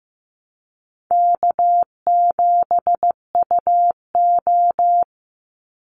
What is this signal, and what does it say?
Morse code, heard as a keyed tone, about 15 wpm, 700 Hz. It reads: K7UO
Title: USA Call Signs 15wpm